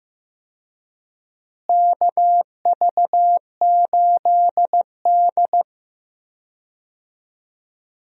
Morse code, heard as a keyed tone, about 15 wpm, 700 Hz. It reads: KV8D
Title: USA Call Signs - 4 Characters - Repeated 2x - 15wpm